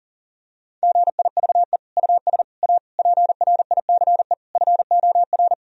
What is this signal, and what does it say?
Morse code, heard as a keyed tone, about 37 wpm, 700 Hz. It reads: GIVE US A PRICE FOR
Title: Sentences from Top 1000 Words - Review Part 4 - 40wpm